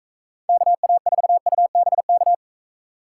Morse code, heard as a keyed tone, about 44 wpm, 700 Hz. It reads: KA4UBK